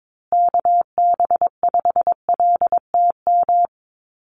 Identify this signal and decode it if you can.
Morse code, heard as a keyed tone, about 22 wpm, 700 Hz. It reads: KB5LTM
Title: USA Call Signs Repeated 2x 22wpm